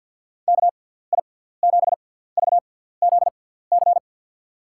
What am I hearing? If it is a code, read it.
Morse code, heard as a keyed tone, about 48 wpm, spaced as for 17 wpm, 700 Hz. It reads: KI7VZC
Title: USA Call Signs - 6 Characters - 6x Character Spacing - 50wpm